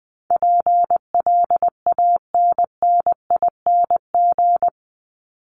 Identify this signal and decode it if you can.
Morse code, heard as a keyed tone, about 20 wpm, 700 Hz. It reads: PLANNING
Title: ING Suffix - Top 100 Words with Base Words 20wpm